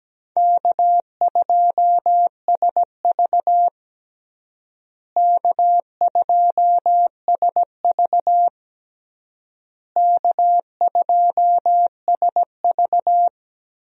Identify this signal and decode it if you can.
Morse code, heard as a keyed tone, about 17 wpm, 700 Hz. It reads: K2SV K2SV K2SV
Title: USA Call Signs - 4 Characters - Repeated 3x - 17wpm